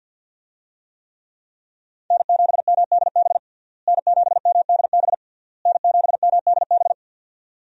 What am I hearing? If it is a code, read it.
Morse code, heard as a keyed tone, about 49 wpm, 700 Hz. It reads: N7MDB N7MDB N7MDB
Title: USA Call Signs - 5 Characters - Repeated 3x - 50wpm